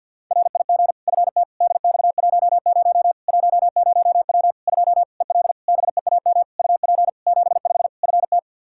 Morse code, heard as a keyed tone, about 51 wpm, 700 Hz. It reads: RIG FT DX10 10W 2 EL BEAM UP 75 FT
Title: K1USN SST QSO Practice 50wpm